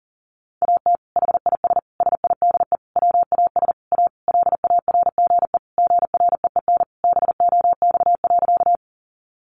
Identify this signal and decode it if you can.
Morse code, heard as a keyed tone, about 40 wpm, 700 Hz. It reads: AT HIS SIDE WAS A LARGE GREEN BOX.
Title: Sentences from Top 500 Words 40wpm